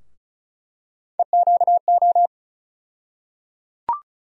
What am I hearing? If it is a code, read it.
Morse code, heard as a keyed tone, about 36 wpm, 700 Hz. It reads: EQO
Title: USA Call Sign Postfixes - 3 Letters 35wpm